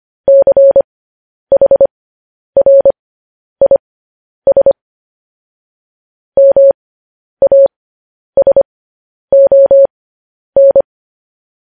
Morse code, heard as a keyed tone, about 25 wpm, spaced as for 10 wpm, 550 Hz. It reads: CHRIS MASON
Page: news headlines in morse code at 10 wpm